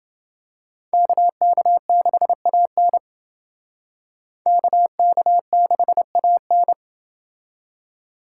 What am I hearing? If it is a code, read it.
Morse code, heard as a keyed tone, about 29 wpm, 700 Hz. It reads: KK6AN KK6AN